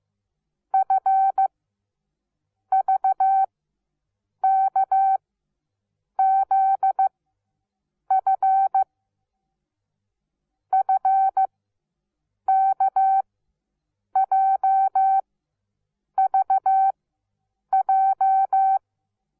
Morse code, heard as a keyed tone, about 15 wpm, spaced as for 7 wpm, 770 Hz. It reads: FVKZF FKJVJ
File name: Onlinelehrgang / morsen / DJ4UF Morsen17